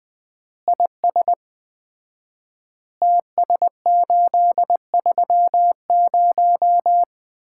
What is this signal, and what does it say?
Morse code, heard as a keyed tone, about 20 wpm, 700 Hz. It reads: IS TS830
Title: Common QSO Phrases - 4x Word-Spacing - 20wpm